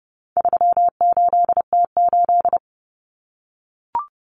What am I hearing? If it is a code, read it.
Morse code, heard as a keyed tone, about 30 wpm, 700 Hz. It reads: 38T8
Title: Common Cut Numbers - 4-Digit 30wpm